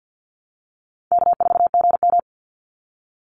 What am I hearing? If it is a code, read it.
Morse code, heard as a keyed tone, about 48 wpm, 700 Hz. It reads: K4GM
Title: Easy Call Signs Repeated 2x - 50wpm